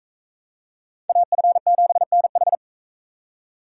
Morse code, heard as a keyed tone, about 42 wpm, 700 Hz. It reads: AF7NH